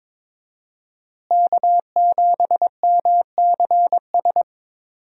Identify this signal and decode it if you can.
Morse code, heard as a keyed tone, about 21 wpm, 700 Hz. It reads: K7MCS
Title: USA Call Signs - 5 Characters - Repeated 3x - 22wpm